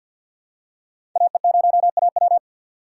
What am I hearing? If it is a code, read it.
Morse code, heard as a keyed tone, about 53 wpm, 700 Hz. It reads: AE0AW
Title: USA Call Signs - 5 Characters - 4x Character Spacing - 50wpm